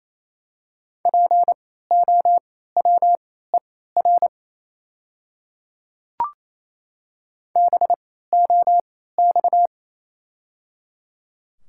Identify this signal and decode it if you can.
Morse code, heard as a keyed tone, about 28 wpm, spaced as for 16 wpm, 700 Hz. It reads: POWER BOX
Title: 3x Character Spacing - Top 500 Words - 28wpm